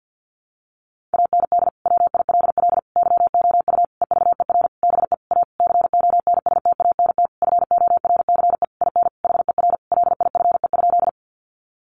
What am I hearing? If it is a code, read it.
Morse code, heard as a keyed tone, about 50 wpm, 700 Hz. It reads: AND WILL YOU EVER BE A CONSTANT FORCE IN HER LIFE?
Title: Sentences from Top 1500 Words 50wpm